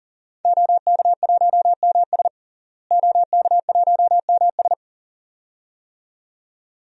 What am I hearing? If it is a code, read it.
Morse code, heard as a keyed tone, about 40 wpm, 700 Hz. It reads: OK1MS OK1MS
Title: DXCC-100 Call Signs - 5 Characters - Repeated 2x - 40wpm